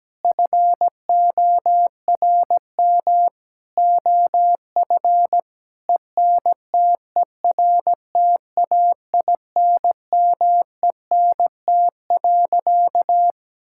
Morse code, heard as a keyed tone, about 17 wpm, 700 Hz. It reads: FORM OF ENTERTAINMENT.
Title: Sentences from Top 1000 Words - Review Part 6 - 17wpm